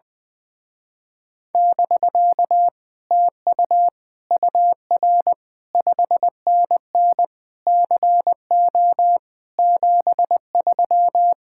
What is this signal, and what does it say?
Morse code, heard as a keyed tone, about 20 wpm, 700 Hz. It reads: <BK> TU UR 5NN CO 73
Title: Speed-Racing Common QSO Phrases Encore - 20wpm